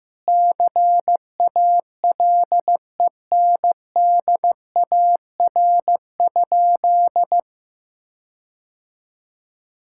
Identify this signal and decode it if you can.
Morse code, heard as a keyed tone, about 15 wpm, 700 Hz. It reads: CALENDAR?
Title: Sentences from Top 1600 Words 15wpm